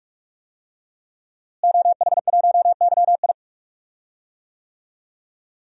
Morse code, heard as a keyed tone, about 41 wpm, 700 Hz. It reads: OH1YI
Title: DXCC-100 Call Signs - 5 Characters - Repeated 2x - 45wpm